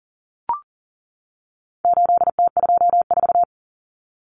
Morse code, heard as a keyed tone, about 40 wpm, 700 Hz. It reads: 8T24
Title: Common Cut Numbers - 4-Digit 40wpm